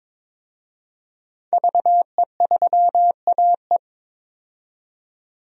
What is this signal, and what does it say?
Morse code, heard as a keyed tone, about 22 wpm, 700 Hz. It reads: VE3AE
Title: DXCC-100 Call Signs - 5 Characters - 22wpm